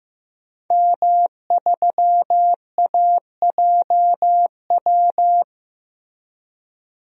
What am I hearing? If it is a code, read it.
Morse code, heard as a keyed tone, about 15 wpm, 700 Hz. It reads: M3AJW
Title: DXCC-100 Call Signs - 5 Characters - Repeated 2x - 15wpm